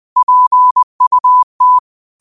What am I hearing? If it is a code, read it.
Morse code, heard as a keyed tone, about 20 wpm, 1000 Hz. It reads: PUT